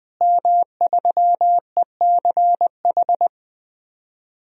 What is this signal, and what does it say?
Morse code, heard as a keyed tone, about 20 wpm, 700 Hz. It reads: M3ECH
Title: DXCC-100 Call Signs - 5 Characters - Repeated 2x - 20wpm